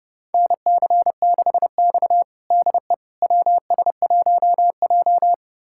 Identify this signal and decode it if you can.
Morse code, heard as a keyed tone, about 30 wpm, 700 Hz. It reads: NC6X DE WS1J
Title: USA Callsign DE Callsign - 4 Characters - 30wpm